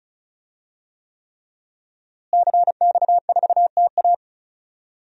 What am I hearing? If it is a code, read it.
Morse code, heard as a keyed tone, about 34 wpm, 700 Hz. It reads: CX4TA
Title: DXCC-100 Call Signs - 5 Characters - Repeated 3x - 35wpm